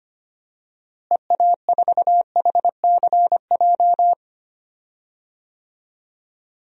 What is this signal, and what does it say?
Morse code, heard as a keyed tone, about 25 wpm, 700 Hz. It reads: EA4HCJ
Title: DXCC-100 Call Signs - 6 Characters - Repeated 2x - 25wpm